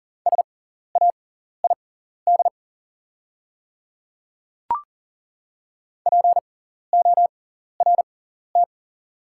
Morse code, heard as a keyed tone, about 40 wpm, spaced as for 14 wpm, 700 Hz. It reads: SAID PORT
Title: 6x Character Spacing - Top 300 Words - 40wpm